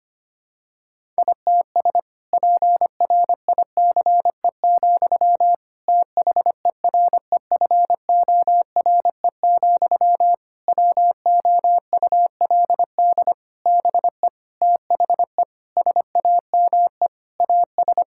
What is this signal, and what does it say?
Morse code, heard as a keyed tone, about 25 wpm, 700 Hz. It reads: ITS PRICE, THEREFORE, WOULD BE THE SAME AS
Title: Sentences from Top 1000 Words - Review Part 5 - 25wpm